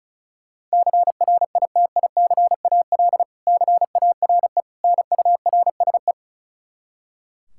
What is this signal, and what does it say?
Morse code, heard as a keyed tone, about 34 wpm, 700 Hz. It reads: CRITICAL CARE NURSE
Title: Practice QSO - Exchange 3 - More Info - 35wpm